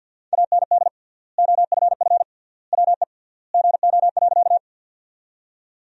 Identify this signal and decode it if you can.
Morse code, heard as a keyed tone, about 50 wpm, 700 Hz. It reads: AND OFF WE GO.
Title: 3x Word-Spacing - Sentences from Top 200 Words - 50wpm